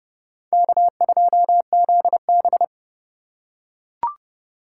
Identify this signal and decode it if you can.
Morse code, heard as a keyed tone, about 28 wpm, 700 Hz. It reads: K2ZB